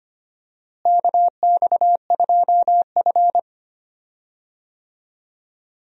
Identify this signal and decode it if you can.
Morse code, heard as a keyed tone, about 25 wpm, 700 Hz. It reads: KX2F
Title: USA Call Signs - 4 Characters - Repeated 2x - 25wpm